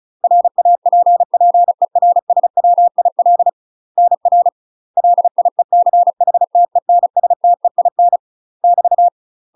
Morse code, heard as a keyed tone, about 36 wpm, 700 Hz. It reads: RAPPERSWIL NR LIECHTENSTEIN <BT>
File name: UK MASTER QSOs TRAINING / 35WPM / UK MASTER QSOs 0033